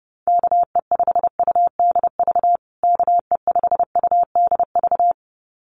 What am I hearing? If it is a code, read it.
Morse code, heard as a keyed tone, about 30 wpm, 700 Hz. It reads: KE5UDV KE5UDV
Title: USA Call Signs Repeated 2x 30wpm